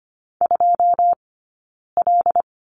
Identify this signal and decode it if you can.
Morse code, heard as a keyed tone, about 25 wpm, 700 Hz. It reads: 2 L
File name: course / 25:10wpm / 092 Lesson 16d - Letter c - Two Characters 25wpm